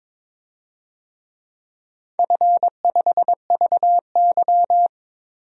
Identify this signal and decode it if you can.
Morse code, heard as a keyed tone, about 21 wpm, 700 Hz. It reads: F5VY
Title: DXCC-100 Call Signs - 4 Characters - Repeated 3x - 22wpm